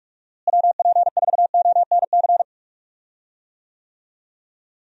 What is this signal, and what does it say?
Morse code, heard as a keyed tone, about 42 wpm, 700 Hz. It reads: WP4ONC
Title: USA Call Signs - 6 Characters - Repeated 3x - 45wpm